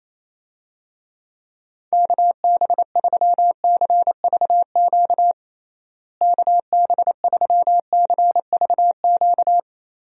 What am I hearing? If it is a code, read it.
Morse code, heard as a keyed tone, about 28 wpm, 700 Hz. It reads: KB3CVQ KB3CVQ
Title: USA Call Signs - 6 Characters - Repeated 3x - 28wpm